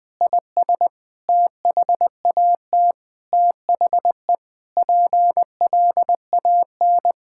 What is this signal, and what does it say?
Morse code, heard as a keyed tone, about 21 wpm, 700 Hz. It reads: IS THAT THE PLAN
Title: Sentences with 9-Word Phrases - Spoken Beginning - 20wpm